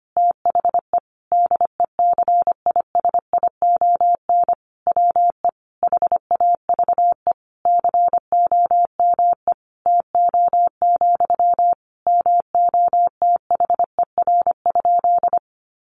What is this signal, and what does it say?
Morse code, heard as a keyed tone, about 25 wpm, 700 Hz. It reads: THE DECISION WE HAVE COME TO, MOTHER?
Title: Sentences from Top 1000 Words 25wpm